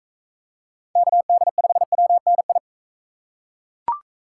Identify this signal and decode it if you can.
Morse code, heard as a keyed tone, about 40 wpm, 700 Hz. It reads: KD5WNI